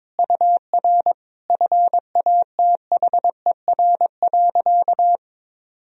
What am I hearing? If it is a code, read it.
Morse code, heard as a keyed tone, about 22 wpm, 700 Hz. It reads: UR FATHER.
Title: Sentences from Top 300 Words and Abbreviations 22wpm